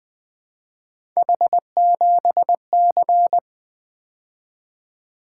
Morse code, heard as a keyed tone, about 24 wpm, 700 Hz. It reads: H7C
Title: Speed-Racing - Three Letter-Number Combinations - 20wpm